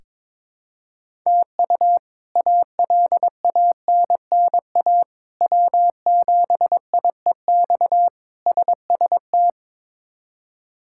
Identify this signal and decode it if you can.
Morse code, heard as a keyed tone, about 22 wpm, 700 Hz. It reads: TU ALANNA W7IEX SST
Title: K1USN SST QSO Practice - 22wpm